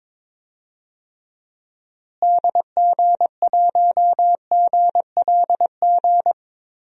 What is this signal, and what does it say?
Morse code, heard as a keyed tone, about 22 wpm, 700 Hz. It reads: DG1GLG
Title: DXCC-100 Call Signs - 6 Characters - Repeated 2x - 22wpm